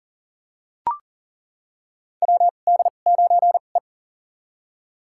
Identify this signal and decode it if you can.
Morse code, heard as a keyed tone, about 40 wpm, spaced as for 29 wpm, 700 Hz. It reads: WD9E
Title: USA Call Signs - 4 Characters - 2x Character Spacing - 40wpm